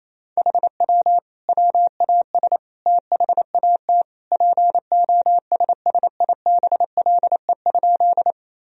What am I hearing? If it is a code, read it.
Morse code, heard as a keyed tone, about 28 wpm, 700 Hz. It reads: HW WAS THAT POSSIBLE?
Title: Sentences from Top 500 Words and Abbreviations 28wpm